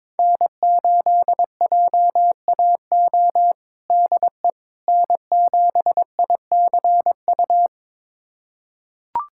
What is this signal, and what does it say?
Morse code, heard as a keyed tone, about 22 wpm, 700 Hz. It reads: N8JAO DE N7ICU
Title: USA Callsign DE Callsign 22wpm